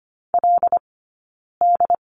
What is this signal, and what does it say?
Morse code, heard as a keyed tone, about 24 wpm, 700 Hz. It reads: L D
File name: course / 25:10wpm / 092 Lesson 16d - Letter c - Two Characters 25wpm